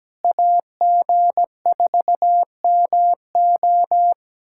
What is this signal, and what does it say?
Morse code, heard as a keyed tone, about 17 wpm, 700 Hz. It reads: AG4MO
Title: SOTA QSO Practice 17wpm